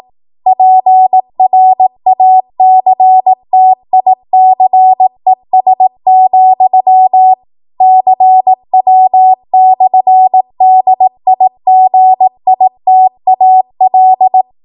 Morse code, heard as a keyed tone, about 18 wpm, 750 Hz. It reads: PRACTICES, CW/DIGITAL
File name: Archive / 18 WPM / 18WPM